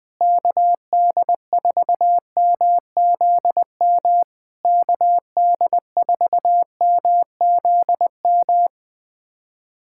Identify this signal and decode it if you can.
Morse code, heard as a keyed tone, about 20 wpm, 700 Hz. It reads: KD4MZM KD4MZM
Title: POTA QSO Practice - 20wpm